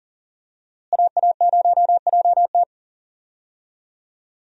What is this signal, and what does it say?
Morse code, heard as a keyed tone, about 37 wpm, 700 Hz. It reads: AA0JT
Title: USA Call Signs - 5 Characters - Repeated 2x - 40wpm